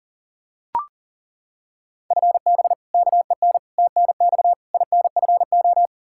Morse code, heard as a keyed tone, about 40 wpm, 700 Hz. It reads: FB KEN TNX INFO
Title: Practice QSO - Exchange 3 - More Info - 40wpm